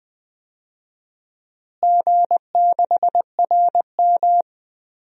Morse code, heard as a keyed tone, about 20 wpm, 700 Hz. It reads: G6RM
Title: DXCC-100 Call Signs - 4 Characters - Repeated 3x - 20wpm